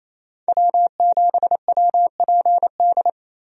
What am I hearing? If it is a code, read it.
Morse code, heard as a keyed tone, about 27 wpm, 700 Hz. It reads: W7WPD
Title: USA Call Signs - 5 Characters - Repeated 2x - 28wpm